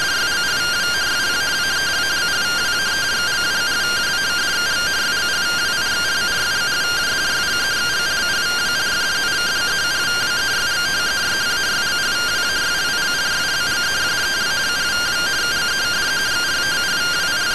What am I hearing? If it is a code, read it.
Two-tone teleprinter signal, about 170 Hz shift.
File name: findings / MR AXIAL 0mm 6mm.gen-06 rtty rx